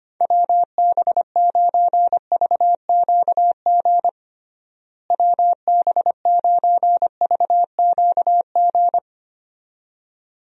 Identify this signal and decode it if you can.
Morse code, heard as a keyed tone, about 25 wpm, 700 Hz. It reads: WB9VQG WB9VQG
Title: USA Call Signs - 6 Characters - Repeated 2x - 25wpm